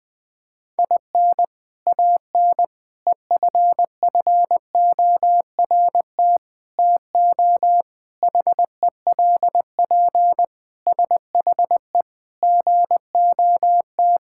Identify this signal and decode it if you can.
Morse code, heard as a keyed tone, about 20 wpm, 700 Hz. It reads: IN AN EFFORT TO HELP SHE GOT
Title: Sentences with 4-Word Phrases - 20wpm